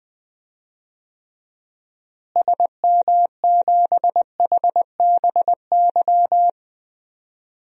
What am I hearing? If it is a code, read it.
Morse code, heard as a keyed tone, about 20 wpm, 700 Hz. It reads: SM7HBY
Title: DXCC-100 Call Signs - 6 Characters - Repeated 3x - 20wpm